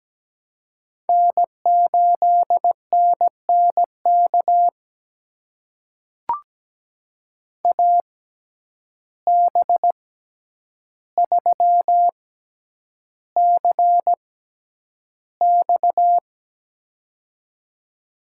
Morse code, heard as a keyed tone, about 17 wpm, spaced as for 9 wpm, 700 Hz. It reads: N8NNK AB3CX
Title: USA Call Signs - 5 Characters - 6x Character Spacing - 17wpm